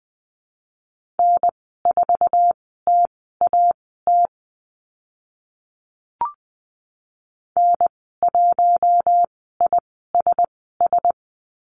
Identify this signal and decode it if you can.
Morse code, heard as a keyed tone, about 20 wpm, spaced as for 14 wpm, 700 Hz. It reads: N4TAT N1ISS
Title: 048 Lesson 9g - Number 4 - Call Signs 20wpm